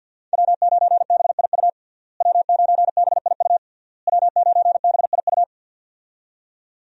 Morse code, heard as a keyed tone, about 53 wpm, 700 Hz. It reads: W9BIU W9BIU W9BIU
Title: USA Call Signs - 5 Characters - Repeated 3x - 50wpm